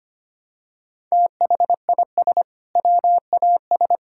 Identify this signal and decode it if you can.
Morse code, heard as a keyed tone, about 25 wpm, 700 Hz. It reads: THIS WAS
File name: practice / 25wpm / Sentences with 2-Word Phrases - 25wpm